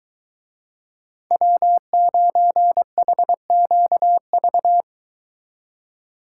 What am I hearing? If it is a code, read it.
Morse code, heard as a keyed tone, about 23 wpm, 700 Hz. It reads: W9HQV